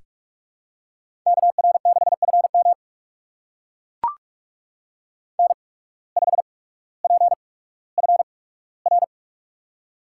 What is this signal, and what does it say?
Morse code, heard as a keyed tone, about 44 wpm, spaced as for 19 wpm, 700 Hz. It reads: KR6FM N5PFR
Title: USA Call Signs - 5 Characters - 8x Character Spacing - 45wpm